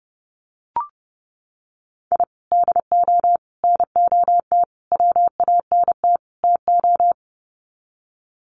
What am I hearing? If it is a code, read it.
Morse code, heard as a keyed tone, about 30 wpm, 700 Hz. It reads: I DO NOT WANT TO
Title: Sets of 5 Words 30wpm